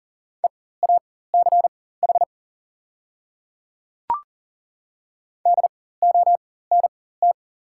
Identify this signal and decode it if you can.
Morse code, heard as a keyed tone, about 39 wpm, spaced as for 19 wpm, 700 Hz. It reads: EACH DONT
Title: 4x Character Spacing - Top 500 Words - 40wpm